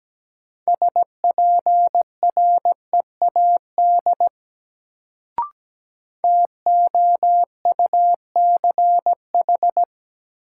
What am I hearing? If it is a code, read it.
Morse code, heard as a keyed tone, about 17 wpm, 700 Hz. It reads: SPREAD TOUCH
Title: Sentences from Top 1200 Words - New Word Drill Alt - 17wpm